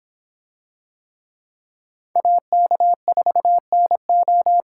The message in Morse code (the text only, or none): AK4NO